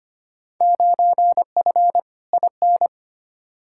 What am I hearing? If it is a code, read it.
Morse code, heard as a keyed tone, about 25 wpm, 700 Hz. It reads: 9F IN
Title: Field Day Exchanges Encore - 25wpm